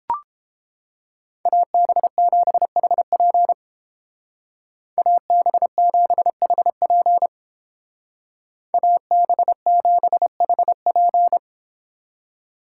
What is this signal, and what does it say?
Morse code, heard as a keyed tone, about 29 wpm, 700 Hz. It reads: AB7HP AB7HP AB7HP